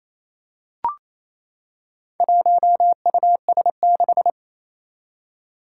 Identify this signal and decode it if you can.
Morse code, heard as a keyed tone, about 28 wpm, 700 Hz. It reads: 1US6
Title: Four Challenging Letter-Number Combinations 28wpm